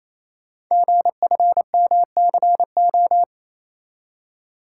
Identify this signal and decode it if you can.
Morse code, heard as a keyed tone, about 28 wpm, 700 Hz. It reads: GFMCO
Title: Code Groups - 5 Letters - 28wpm